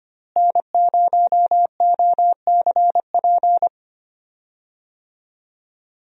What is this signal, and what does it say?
Morse code, heard as a keyed tone, about 25 wpm, 700 Hz. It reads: N0OCP
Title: USA Call Signs - 5 Characters - Repeated 3x - 25wpm